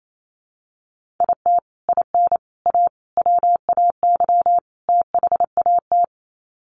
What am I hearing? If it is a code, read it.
Morse code, heard as a keyed tone, about 28 wpm, 700 Hz. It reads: IT IN A WAY THAT
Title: Sets of 5 Words Encore2 28wpm